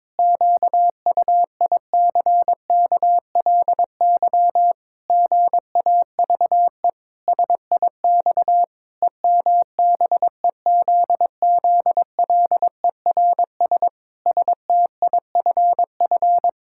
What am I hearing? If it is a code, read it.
Morse code, heard as a keyed tone, about 22 wpm, 700 Hz. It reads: QUICKLY GAVE SIX EMBEZZLERS STIFF
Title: Pangram Challenge v2 - 22wpm